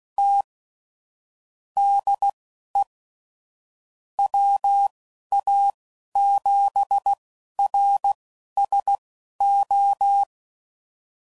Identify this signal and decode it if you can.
Morse code, heard as a keyed tone, about 16 wpm, spaced as for 11 wpm, 800 Hz. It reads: T DE WA7RSO
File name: CW Code Groups @15WPM / QSOs / WD0FBT Spaced